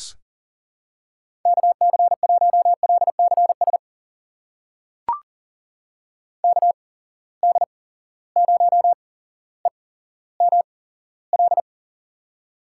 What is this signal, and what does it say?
Morse code, heard as a keyed tone, about 37 wpm, spaced as for 17 wpm, 700 Hz. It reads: KC1LCS KD0EML
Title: USA Call Signs - 6 Characters - 8x Character Spacing - 40wpm